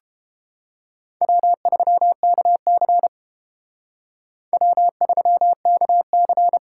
Morse code, heard as a keyed tone, about 29 wpm, 700 Hz. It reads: W3KC W3KC